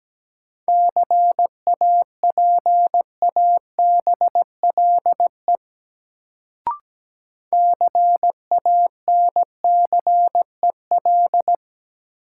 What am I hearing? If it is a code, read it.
Morse code, heard as a keyed tone, about 17 wpm, 700 Hz. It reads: CAPABLE CANCEL
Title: Sentences from Top 1200 Words - New Word Drill Alt - 17wpm